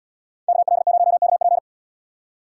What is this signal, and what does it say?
Morse code, heard as a keyed tone, about 72 wpm, 700 Hz. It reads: KL0KP